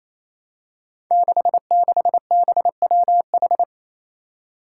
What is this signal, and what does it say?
Morse code, heard as a keyed tone, about 29 wpm, 700 Hz. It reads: 66BWH
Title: Five Challenging Letter-Number Combinations 28wpm